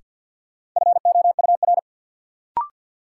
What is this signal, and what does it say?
Morse code, heard as a keyed tone, about 53 wpm, 700 Hz. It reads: FOUR
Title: Sentences from Top 300 Words - New Word Drill - 50wpm